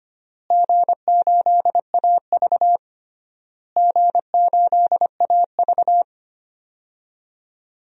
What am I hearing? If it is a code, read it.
Morse code, heard as a keyed tone, about 25 wpm, 700 Hz. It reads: G8AV G8AV
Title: DXCC-100 Call Signs - 4 Characters - Repeated 3x - 25wpm